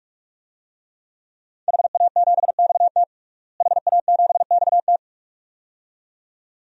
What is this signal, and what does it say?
Morse code, heard as a keyed tone, about 46 wpm, 700 Hz. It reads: HA7XT HA7XT
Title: DXCC-100 Call Signs - 5 Characters - Repeated 2x - 45wpm